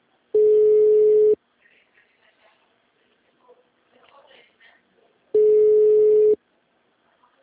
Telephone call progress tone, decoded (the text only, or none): ringback tone